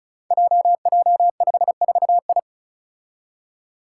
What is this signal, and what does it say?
Morse code, heard as a keyed tone, about 36 wpm, 700 Hz. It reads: JJ54I
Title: Five Challenging Letter-Number Combinations 35wpm